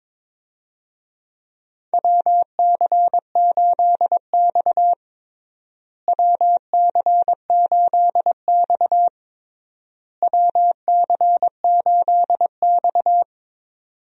Morse code, heard as a keyed tone, about 22 wpm, 700 Hz. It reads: WC8X WC8X WC8X
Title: USA Call Signs - 4 Characters - Repeated 3x - 22wpm